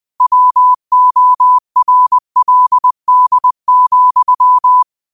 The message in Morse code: WORLD,

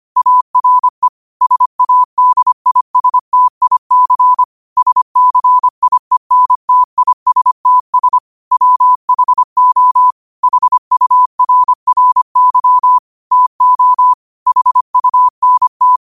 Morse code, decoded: ARE SADISTIC SCIENTISTS WHO HURRY TO HUNT

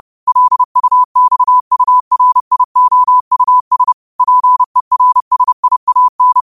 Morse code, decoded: LUXURIOUS PERSIAN

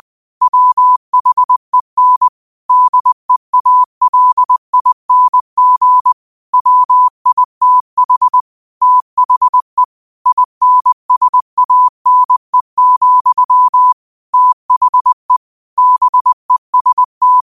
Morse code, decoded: WHEN DEALING WITH THE INSANE, THE BEST

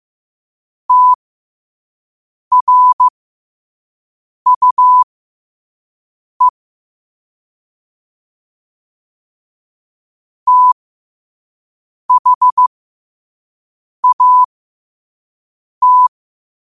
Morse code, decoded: TRUE THAT

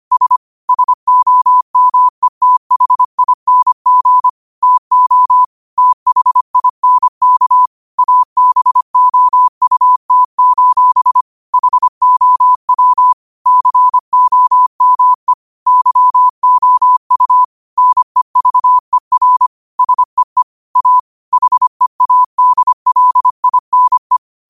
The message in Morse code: S SOMETHING TO THINK ABOUT: HOW COME YOU NEVER SEE A HEADLINE

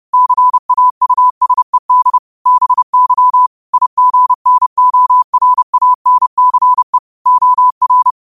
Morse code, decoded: CAUSED BY IGNORANCE OR

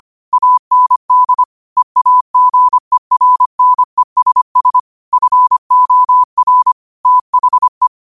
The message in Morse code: AND EAGERNESS FOR THE

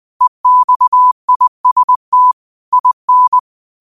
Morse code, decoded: EXIST IN

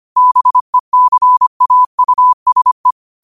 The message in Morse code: BECAUSE

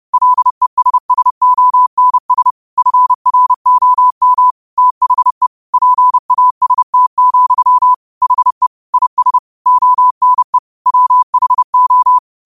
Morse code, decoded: LESSONS FROM THE PAST, HE IS ONE WHO

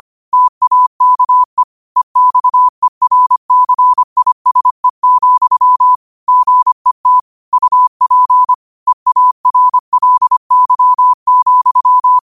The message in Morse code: TAKE EXERCISE, GET UP EARLY,

